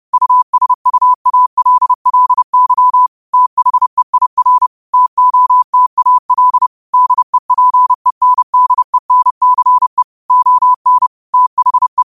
Morse code, heard as a keyed tone, about 30 wpm, 1000 Hz. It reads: USUALLY THEIR TOTAL DEPENDENCE ON THE